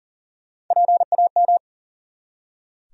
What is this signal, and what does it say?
Morse code, heard as a keyed tone, about 40 wpm, 700 Hz. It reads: PAM
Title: USA Common Names - Max 3 Letters - 40wpm